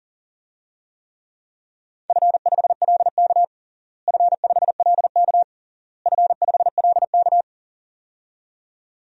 Morse code, heard as a keyed tone, about 40 wpm, 700 Hz. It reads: F5LK F5LK F5LK
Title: DXCC-100 Call Signs - 4 Characters - Repeated 3x - 40wpm